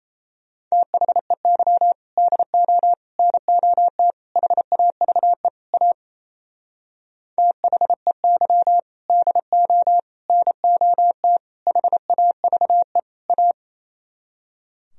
Morse code, heard as a keyed tone, about 34 wpm, 700 Hz. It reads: THEY DO NOT HAVE A THEY DO NOT HAVE A
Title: Speed-Racing - Sets of 5 Words 28wpm